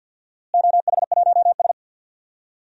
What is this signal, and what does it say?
Morse code, heard as a keyed tone, about 50 wpm, 700 Hz. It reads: OH1S
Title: DXCC-100 Call Signs - 4 Characters - Repeated 3x - 50wpm